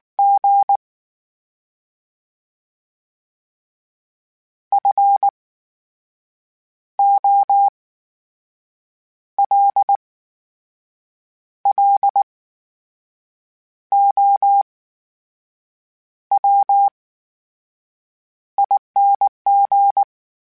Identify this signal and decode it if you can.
Morse code, heard as a keyed tone, about 19 wpm, spaced as for 6 wpm, 800 Hz. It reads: G FOLLOWING
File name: Special INGs / Special INGs-mp3s / VerySlowToFaster Ing @18